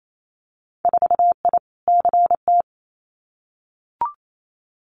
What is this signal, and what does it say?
Morse code, heard as a keyed tone, about 28 wpm, 700 Hz. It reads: 4I CT
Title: Winter Field Day Exchanges Encore 28wpm